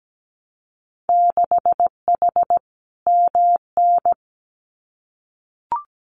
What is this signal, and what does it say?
Morse code, heard as a keyed tone, about 17 wpm, 700 Hz. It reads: 6H MN
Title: Winter Field Day Exchanges Encore 17wpm